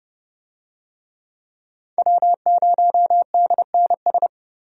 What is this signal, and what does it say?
Morse code, heard as a keyed tone, about 30 wpm, 700 Hz. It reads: W0DNS
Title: USA Call Signs - 5 Characters - 30wpm